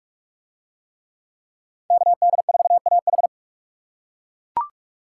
Morse code, heard as a keyed tone, about 39 wpm, 700 Hz. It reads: KD4AH